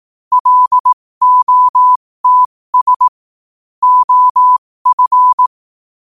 Morse code, heard as a keyed tone, about 18 wpm, spaced as for 15 wpm, 1000 Hz. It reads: LOTS OF